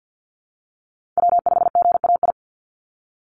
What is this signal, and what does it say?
Morse code, heard as a keyed tone, about 48 wpm, 700 Hz. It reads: W5GAI
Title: Call Signs Repeated 2x - 50wpm